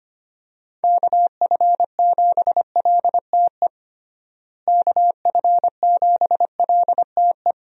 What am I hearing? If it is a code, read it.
Morse code, heard as a keyed tone, about 25 wpm, 700 Hz. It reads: KF7LTE KF7LTE
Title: USA Call Signs - 6 Characters - Repeated 2x - 25wpm